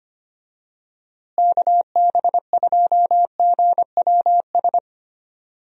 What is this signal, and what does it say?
Morse code, heard as a keyed tone, about 25 wpm, 700 Hz. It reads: KB2GWS
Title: USA Call Signs - 6 Characters - Repeated 2x - 25wpm